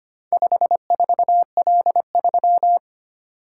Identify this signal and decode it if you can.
Morse code, heard as a keyed tone, about 25 wpm, 700 Hz. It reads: 54L3
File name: practice / 25wpm / Four Letter-Number Combinations - 25wpm